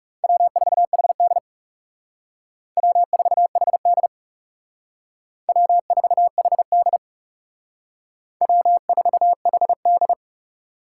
Morse code, heard as a keyed tone, about 36 wpm, 700 Hz. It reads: W4HD W4HD W4HD W4HD
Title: Speed Racing - USA 1x2 Call Signs 30wpm